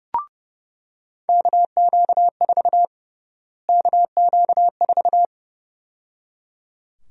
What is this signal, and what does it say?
Morse code, heard as a keyed tone, about 32 wpm, 700 Hz. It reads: KQ4 KQ4
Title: USA Call Sign Prefixes - Repeated 2x - 30wpm